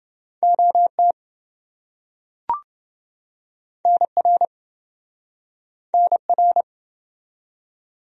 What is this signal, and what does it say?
Morse code, heard as a keyed tone, about 28 wpm, 700 Hz. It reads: OT NR NR